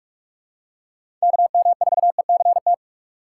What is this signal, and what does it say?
Morse code, heard as a keyed tone, about 43 wpm, 700 Hz. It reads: KM4ECT
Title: USA Call Signs - 6 Characters - 6x Character Spacing - 45wpm